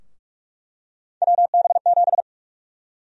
Morse code, heard as a keyed tone, about 47 wpm, 700 Hz. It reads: WB7
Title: USA Call Sign Prefixes - Repeated 2x - 45wpm